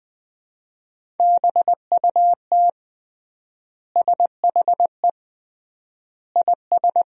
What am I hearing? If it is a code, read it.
Morse code, heard as a keyed tone, about 20 wpm, 700 Hz. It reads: BUT SHE IS
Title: 3x Word-Spacing - Sentences from Top 500 Words - 20wpm